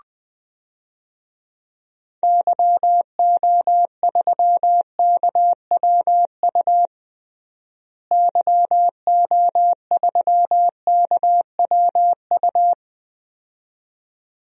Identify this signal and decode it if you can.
Morse code, heard as a keyed tone, about 20 wpm, 700 Hz. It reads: YO3KWU YO3KWU
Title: DXCC-100 Call Signs - 6 Characters - Repeated 2x - 20wpm